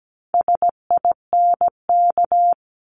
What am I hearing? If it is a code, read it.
Morse code, heard as a keyed tone, about 17 wpm, 700 Hz. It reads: SINK
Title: Multi-letter Phonemes 17wpm